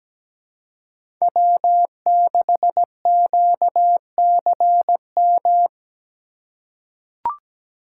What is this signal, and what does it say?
Morse code, heard as a keyed tone, about 17 wpm, 700 Hz. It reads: W6QCM
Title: USA Call Signs - 5 Characters - 2x Character Spacing - 17wpm